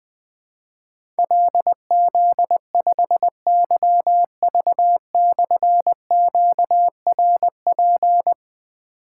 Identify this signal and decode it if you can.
Morse code, heard as a keyed tone, about 20 wpm, 700 Hz. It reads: LZ5YV/QRP
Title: Difficult Call Signs 20wpm